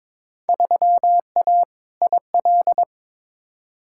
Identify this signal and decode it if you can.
Morse code, heard as a keyed tone, about 22 wpm, 700 Hz. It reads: 3A IL
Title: Field Day Exchanges - 22wpm